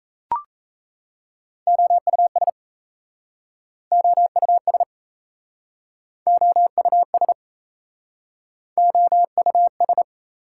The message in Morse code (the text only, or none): OUS OUS OUS OUS